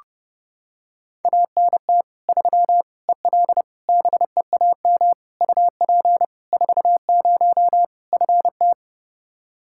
Morse code, heard as a keyed tone, about 32 wpm, 700 Hz. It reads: ANT 3 EL BEAM UP 40 FT
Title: Practice QSO - Exchange 2 - Info - 30wpm